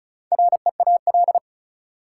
Morse code, heard as a keyed tone, about 36 wpm, 700 Hz. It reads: REAL
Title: Four Letter Words - 35wpm